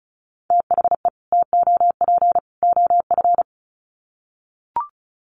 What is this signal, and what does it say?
Morse code, heard as a keyed tone, about 34 wpm, 700 Hz. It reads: THE TOP OF
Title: Sets of 3 Words 35wpm